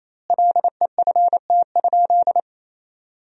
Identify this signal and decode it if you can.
Morse code, heard as a keyed tone, about 28 wpm, 700 Hz. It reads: LEFT?
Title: 2x Word-Spacing - Sentences from Top 300 Words - 28wpm